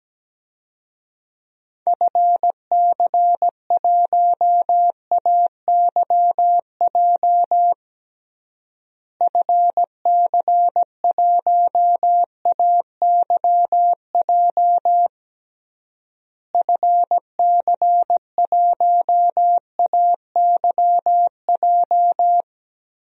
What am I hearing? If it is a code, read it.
Morse code, heard as a keyed tone, about 17 wpm, 700 Hz. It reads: FC1AYJ FC1AYJ FC1AYJ
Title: DXCC-100 Call Signs - 6 Characters - Repeated 3x - 17wpm